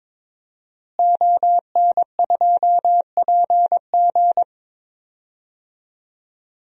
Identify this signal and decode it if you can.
Morse code, heard as a keyed tone, about 21 wpm, 700 Hz. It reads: ON2PG
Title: DXCC-100 Call Signs - 5 Characters - Repeated 3x - 22wpm